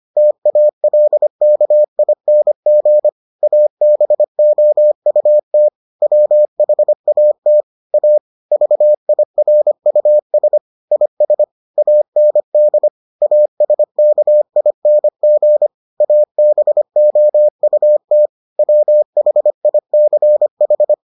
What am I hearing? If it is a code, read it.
Morse code, heard as a keyed tone, about 25 wpm, 600 Hz. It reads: TALKING ABOUT WHAT A VIRUS IS AND ASKING ABOUT WHICH